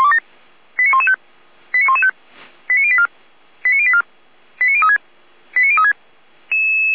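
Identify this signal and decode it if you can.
Five-tone selective call (ZVEI), heard as quick runs of five tones, tones of about 70 ms. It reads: 89115 89115 89084 89084 89036 89036
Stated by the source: Received by SDR.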